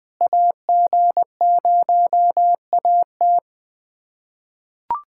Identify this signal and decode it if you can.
Morse code, heard as a keyed tone, about 20 wpm, 700 Hz. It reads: AG0AT
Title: USA Callsign DE Callsign - 5 Characters - 20wpm